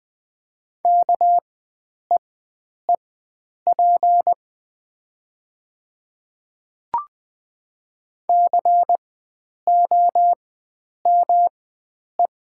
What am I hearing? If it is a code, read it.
Morse code, heard as a keyed tone, about 20 wpm, spaced as for 9 wpm, 700 Hz. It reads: KEEP COME
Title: 4x Character Spacing - Top 300 Words - 20wpm